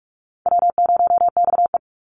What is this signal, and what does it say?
Morse code, heard as a keyed tone, about 46 wpm, 700 Hz. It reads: W0XE
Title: Call Signs - 45wpm